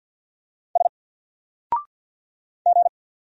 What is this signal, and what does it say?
Morse code, heard as a keyed tone, about 50 wpm, 700 Hz. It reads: S G
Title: Single Letter-Number Alt 50wpm